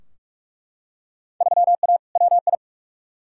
Morse code, heard as a keyed tone, about 42 wpm, 700 Hz. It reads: 3A WI
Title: Field Day Exchanges - 45wpm